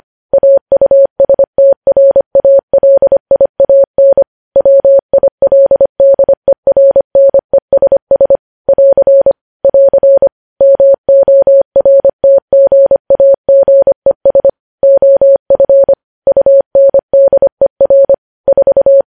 Morse code, heard as a keyed tone, about 25 wpm, 550 Hz. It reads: AUSTRALIAN WILDERNESS <AR> <AR> MORTGAGES OF UNDER 4
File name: podcast / News headlines-1x-25x25